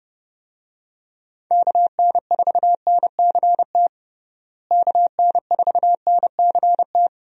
Morse code, heard as a keyed tone, about 30 wpm, 700 Hz. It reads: KN4NCT KN4NCT
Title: USA Call Signs - 6 Characters - Repeated 2x - 30wpm